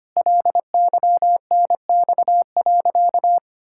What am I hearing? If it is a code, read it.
Morse code, heard as a keyed tone, about 25 wpm, 700 Hz. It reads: LYNX.
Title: Pangram Challenge v2 - 25wpm